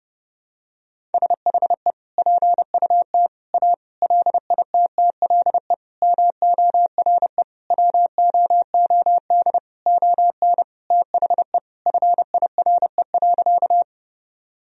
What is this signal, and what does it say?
Morse code, heard as a keyed tone, about 32 wpm, 700 Hz. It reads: SHE PUT A LITTLE MORE WOOD ON THE FIRE.
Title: Sentences from Top 1000 Words - Review Part 2 - 30wpm